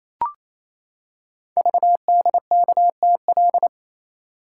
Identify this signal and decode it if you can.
Morse code, heard as a keyed tone, about 27 wpm, 700 Hz. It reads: VDKTL
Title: Code Groups - 5 Letters - 28wpm